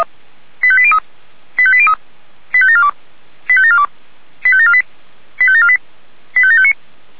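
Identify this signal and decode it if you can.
Five-tone selective call (ZVEI), heard as quick runs of five tones, tones of about 70 ms. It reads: 85803 85803 85742 85742 85748 85748 85759
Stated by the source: Five-tone ZVEI selective call received on an SDR.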